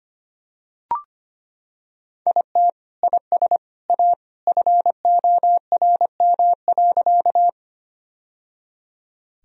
Morse code, heard as a keyed tone, about 25 wpm, 700 Hz. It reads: IT IS A FORM.
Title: Sentences from Top 1000 Words - Review Part 1 - 25wpm